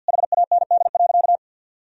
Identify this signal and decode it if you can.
Morse code, heard as a keyed tone, about 46 wpm, 700 Hz. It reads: HAND.
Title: Sentences from Top 1000 Words - Review Part 3 - 50wpm